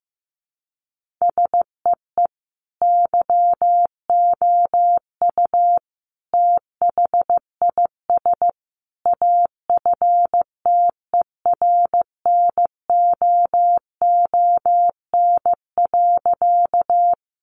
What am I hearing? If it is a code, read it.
Morse code, heard as a keyed tone, about 15 wpm, 700 Hz. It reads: SEE YOU THIS AFTERNOON.
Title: Sentences from Top 1500 Words 15wpm